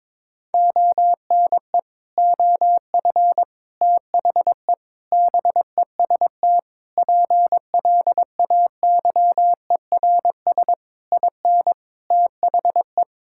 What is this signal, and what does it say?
Morse code, heard as a keyed tone, about 22 wpm, 700 Hz. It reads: ONE OF THE BEST PLAYERS IN THE
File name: practice / 22wpm / Sets of 7 Words - 22wpm